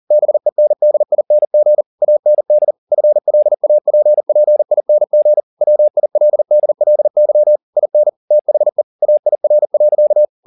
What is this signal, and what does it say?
Morse code, heard as a keyed tone, about 40 wpm, 600 Hz. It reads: BENDING AND FLAPPING WILDLY IN THE AIR.
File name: the old man and the sea 2 40wpm